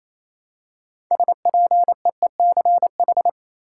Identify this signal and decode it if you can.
Morse code, heard as a keyed tone, about 28 wpm, 700 Hz. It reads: SPEECH
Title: Six Letter Words - 28wpm